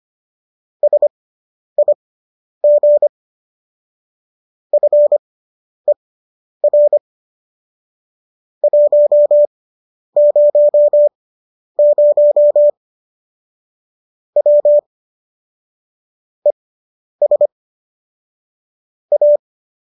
Morse code, heard as a keyed tone, about 25 wpm, spaced as for 10 wpm, 590 Hz. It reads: SIG FER 100 W ES A